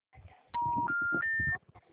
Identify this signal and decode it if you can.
Special information tone.